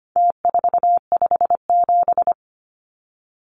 Morse code, heard as a keyed tone, about 25 wpm, 700 Hz. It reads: T457
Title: Numbers - Contest Style - 4-Digit 25wpm